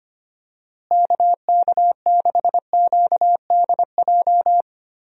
Morse code, heard as a keyed tone, about 25 wpm, 700 Hz. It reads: KK6QDJ
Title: USA Call Signs - 6 Characters - Repeated 3x - 25wpm